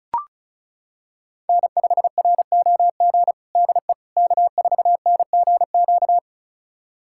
Morse code, heard as a keyed tone, about 36 wpm, 700 Hz. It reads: N5ROG DE K4NGQ
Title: USA Callsign DE Callsign - 5 Characters - 35wpm